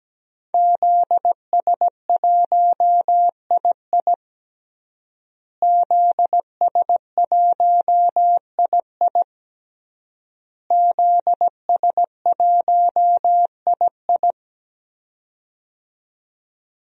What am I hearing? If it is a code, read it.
Morse code, heard as a keyed tone, about 17 wpm, 700 Hz. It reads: ZS1II ZS1II ZS1II
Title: DXCC-100 Call Signs - 5 Characters - Repeated 3x - 17wpm